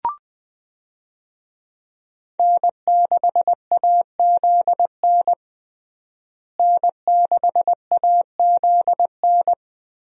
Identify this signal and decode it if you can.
Morse code, heard as a keyed tone, about 20 wpm, 700 Hz. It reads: N6AZN N6AZN
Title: USA Call Signs - 5 Characters - Repeated 2x - 20wpm